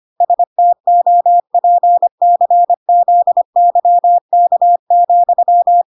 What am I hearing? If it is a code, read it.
Morse code, heard as a keyed tone, about 25 wpm, 700 Hz. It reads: STOPCZYK,